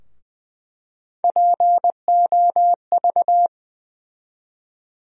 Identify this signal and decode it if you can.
Morse code, heard as a keyed tone, about 20 wpm, 700 Hz. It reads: POV
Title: USA Call Sign Postfixes - 3 Letters 20wpm